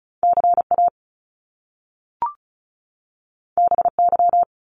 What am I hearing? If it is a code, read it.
Morse code, heard as a keyed tone, about 36 wpm, 700 Hz. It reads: C A B Y
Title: Top Two Letter Combinations in Words 35wpm